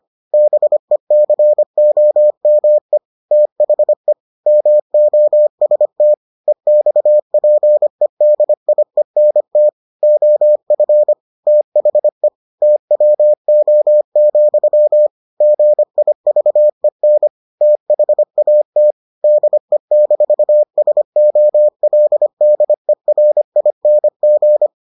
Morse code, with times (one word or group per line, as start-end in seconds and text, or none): 0.3-3.0 BECOME
3.3-4.1 THE
4.5-6.1 MOST
6.5-9.7 EXPEDIENT
10.0-11.1 OF
11.5-12.3 THE
12.6-15.1 TWO,
15.4-17.3 GIVEN
17.6-18.9 THAT
19.2-24.7 DE-SOLDERING